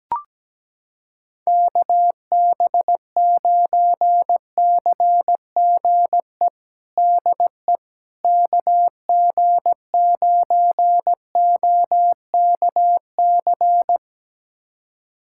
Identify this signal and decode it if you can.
Morse code, heard as a keyed tone, about 17 wpm, 700 Hz. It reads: KB9CGE DE KG9OKC
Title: USA Callsign DE Callsign - 6 Characters - 17wpm